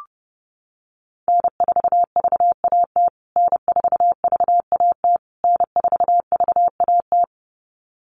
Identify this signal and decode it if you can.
Morse code, heard as a keyed tone, about 30 wpm, 700 Hz. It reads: N4VAT N4VAT N4VAT
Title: USA Call Signs Repeated 3x 30wpm